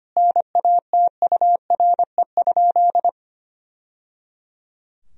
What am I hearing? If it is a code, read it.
Morse code, heard as a keyed tone, about 25 wpm, 700 Hz. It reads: NATURE?
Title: Sentences from Top 1000 Words - Review Part 3 - 25wpm